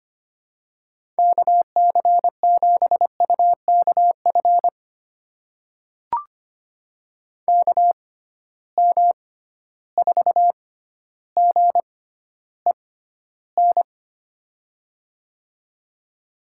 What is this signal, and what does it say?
Morse code, heard as a keyed tone, about 25 wpm, spaced as for 13 wpm, 700 Hz. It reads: KC7UKF KM4GEN
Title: USA Call Signs - 6 Characters - 6x Character Spacing - 25wpm